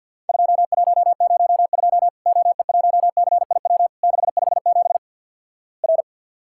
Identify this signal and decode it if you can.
Morse code, heard as a keyed tone, about 49 wpm, 700 Hz. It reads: 2102 OE1CIW 657 R
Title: Worked All Europe DX Contest - QTC Exchange - 50wpm